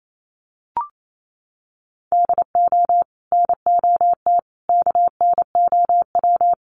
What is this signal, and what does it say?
Morse code, heard as a keyed tone, about 28 wpm, 700 Hz. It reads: DO NOT KNOW
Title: Sets of 5 Words 28wpm